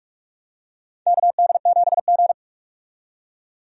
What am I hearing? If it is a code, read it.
Morse code, heard as a keyed tone, about 43 wpm, 700 Hz. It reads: KD7G
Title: USA Call Signs - 4 Characters - 45wpm